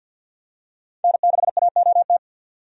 Morse code, heard as a keyed tone, about 47 wpm, 700 Hz. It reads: N6AOT
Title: USA Call Signs - 5 Characters - 50wpm